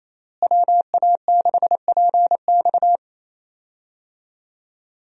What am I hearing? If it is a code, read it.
Morse code, heard as a keyed tone, about 28 wpm, 700 Hz. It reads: WA6PX
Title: USA Call Signs - 5 Characters - Repeated 3x - 28wpm